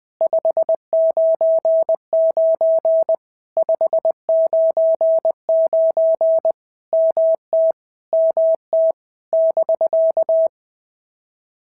Morse code, heard as a keyed tone, about 19 wpm, 650 Hz. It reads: 599 599 MT MT <BK>
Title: POTA QSO Practice - 20wpm